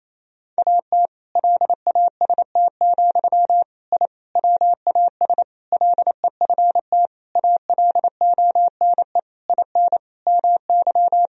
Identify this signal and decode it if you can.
Morse code, heard as a keyed tone, about 28 wpm, 700 Hz. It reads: AT LAST, I WAS LEFT ALONE IN MY
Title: Sentences from Top 1000 Words - Review Part 1 - 28wpm